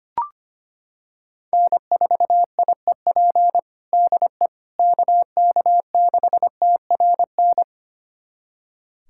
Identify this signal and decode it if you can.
Morse code, heard as a keyed tone, about 25 wpm, 700 Hz. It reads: N4IEP DE KK6TRN
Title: USA Callsign DE Callsign 25wpm